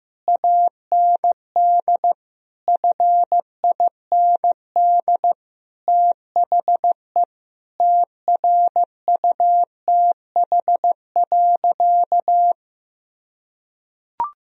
Morse code, heard as a keyed tone, about 15 wpm, 700 Hz. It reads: AND FIND THE TRUTH.
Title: Sentences from Top 1000 Words - Review Part 3 - 15wpm